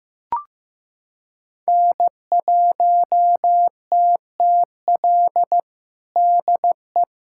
Morse code, heard as a keyed tone, about 16 wpm, 700 Hz. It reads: N1TTL DE
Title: USA Callsign DE Callsign - 5 Characters - 15wpm